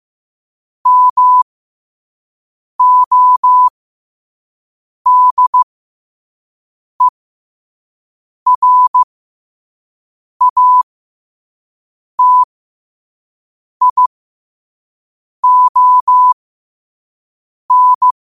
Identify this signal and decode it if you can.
Morse code, heard as a keyed tone, about 15 wpm, spaced as for 5 wpm, 1000 Hz. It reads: MODERATION
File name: 5wpm